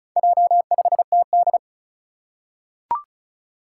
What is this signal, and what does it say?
Morse code, heard as a keyed tone, about 36 wpm, 690 Hz. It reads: J5TD
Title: Four Challenging Letter-Number Combinations 35wpm